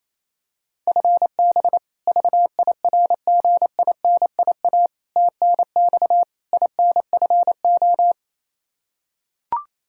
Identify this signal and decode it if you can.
Morse code, heard as a keyed tone, about 28 wpm, 700 Hz. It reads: FB VIRGINIA TNX INFO
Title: K1USN SST QSO Practice 28wpm